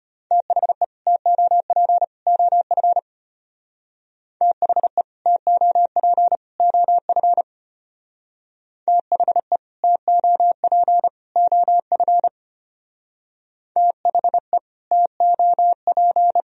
Morse code, heard as a keyed tone, about 33 wpm, 700 Hz. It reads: THE TOP OF THE TOP OF THE TOP OF THE TOP